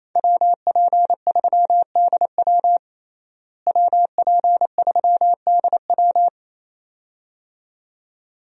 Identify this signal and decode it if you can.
Morse code, heard as a keyed tone, about 28 wpm, 700 Hz. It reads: WP3DW WP3DW
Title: USA Call Signs - 5 Characters - Repeated 2x - 28wpm